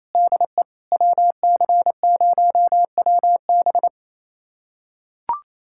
Morse code, heard as a keyed tone, about 28 wpm, 700 Hz. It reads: DE WC0WB
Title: USA Callsign DE Callsign - 5 Characters - 28wpm